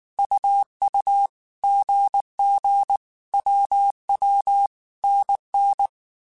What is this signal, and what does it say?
Morse code, heard as a keyed tone, about 19 wpm, 800 Hz. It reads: UU GG WW NN